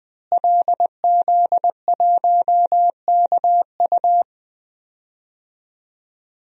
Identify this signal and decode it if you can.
Morse code, heard as a keyed tone, about 20 wpm, 700 Hz. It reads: LZ1KU
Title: DXCC-100 Call Signs - 5 Characters - Repeated 3x - 20wpm